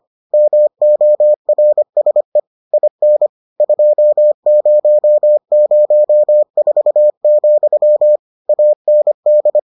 Morse code, heard as a keyed tone, about 25 wpm, 600 Hz. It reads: MORSE IN 2004, AND